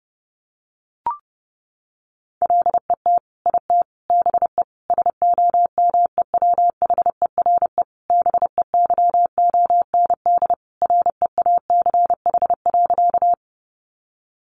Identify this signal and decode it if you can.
Morse code, heard as a keyed tone, about 30 wpm, 700 Hz. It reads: LET IT BE SOMEWHERE BEYOND REACH.
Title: Sentences from Top 1400 Words 30wpm